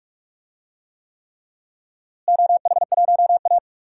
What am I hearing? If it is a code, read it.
Morse code, heard as a keyed tone, about 41 wpm, 700 Hz. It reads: OH1A
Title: DXCC-100 Call Signs - 4 Characters - Repeated 3x - 45wpm